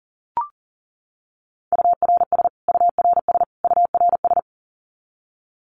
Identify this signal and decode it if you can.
Morse code, heard as a keyed tone, about 37 wpm, 700 Hz. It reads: URS URS URS
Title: Abbreviations Repeated 3x - 40wpm